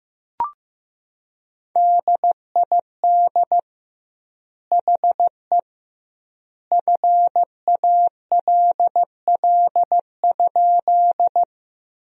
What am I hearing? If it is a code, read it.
Morse code, heard as a keyed tone, about 16 wpm, 700 Hz. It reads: DID HE FALL?
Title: 2x Word-Spacing - Sentences from Top 500 Words - 15wpm